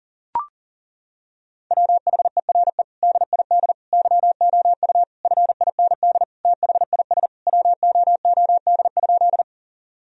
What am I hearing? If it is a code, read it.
Morse code, heard as a keyed tone, about 40 wpm, 700 Hz. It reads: WHERE DID YOU FIND THIS WOOD?
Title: Sentences from Top 1000 Words - Review Part 2 - 40wpm